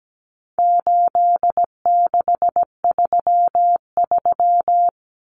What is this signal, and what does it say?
Morse code, heard as a keyed tone, about 17 wpm, 700 Hz. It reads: 8633
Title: Numbers - 4-Digit 17wpm